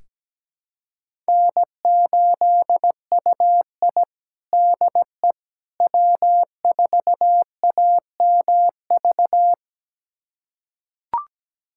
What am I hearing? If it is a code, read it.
Morse code, heard as a keyed tone, about 17 wpm, 700 Hz. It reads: N8UI DE W4AMV
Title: USA Callsign DE Callsign 17wpm